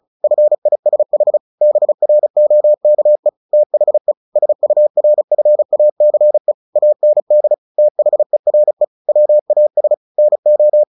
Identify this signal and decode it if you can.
Morse code, heard as a keyed tone, about 34 wpm, 600 Hz. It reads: FISH BROKE THE SURFACE AND THERE WAS NO
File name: the old man and the sea / the old man and the sea 1 35wpm